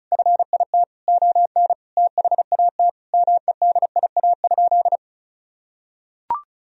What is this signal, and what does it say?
Morse code, heard as a keyed tone, about 34 wpm, 700 Hz. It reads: FIT ON THAT MEDIA?
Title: Sentences from Top 1000 Words - Review Part 4 - 35wpm